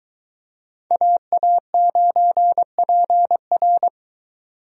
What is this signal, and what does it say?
Morse code, heard as a keyed tone, about 23 wpm, 700 Hz. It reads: AA9PR